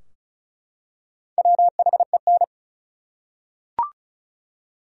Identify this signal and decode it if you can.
Morse code, heard as a keyed tone, about 36 wpm, 700 Hz. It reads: WHEN
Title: Top 100 Words 35wpm